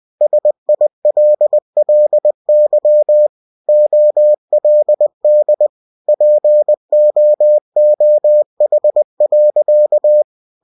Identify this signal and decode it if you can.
Morse code, heard as a keyed tone, about 20 wpm, 600 Hz. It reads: SILLY OLD POOH.
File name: cw-books / winnie-the-pooh / winnie-20wpm-page-0108